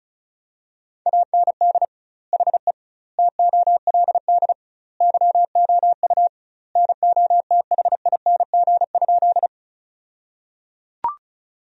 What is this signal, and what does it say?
Morse code, heard as a keyed tone, about 36 wpm, 700 Hz. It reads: AND HE TOLD YOU NOTHING?
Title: 2x Word-Spacing - Sentences from Top 500 Words - 35wpm